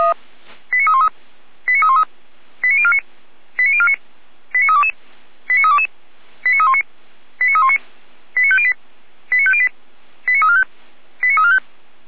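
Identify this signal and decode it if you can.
Five-tone selective call (ZVEI), heard as quick runs of five tones, tones of about 70 ms. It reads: 89313 89313 89049 89049 89322 89322 89319 89319 89508 89508 89345 89345
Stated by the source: ZVEI five-tone selective call received by SDR, recorded by Feuerwehr Orlamünde.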